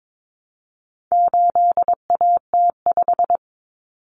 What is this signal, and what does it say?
Morse code, heard as a keyed tone, about 22 wpm, 700 Hz. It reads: 8AT5
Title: Common Cut Numbers - 4-Digit 22wpm